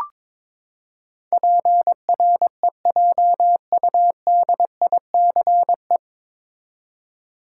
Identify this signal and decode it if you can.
Morse code, heard as a keyed tone, about 21 wpm, 700 Hz. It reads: PREJUDICE
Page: Morse Code Ninja practice 22wpm